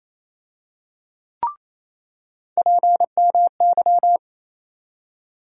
Morse code, heard as a keyed tone, about 28 wpm, 700 Hz. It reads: PMY
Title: USA Call Sign Postfixes - 3 Letters 28wpm